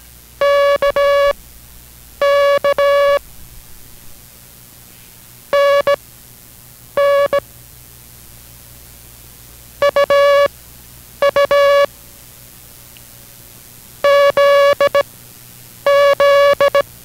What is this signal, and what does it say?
Morse code, heard as a keyed tone, about 18 wpm, spaced as for 8 wpm, 570 Hz. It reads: KK NN UU ZZ